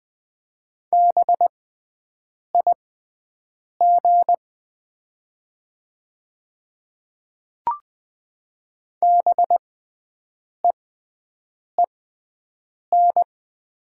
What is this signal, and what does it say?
Morse code, heard as a keyed tone, about 20 wpm, spaced as for 7 wpm, 700 Hz. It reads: BIG BEEN
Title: 6x Character Spacing - Top 500 Words - 20wpm